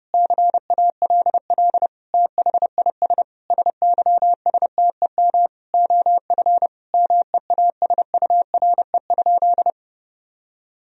Morse code, heard as a keyed tone, about 30 wpm, 700 Hz. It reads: CALL THIS SYSTEM OF MEASURE?
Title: Sentences from Top 1000 Words - Review Part 3 - 30wpm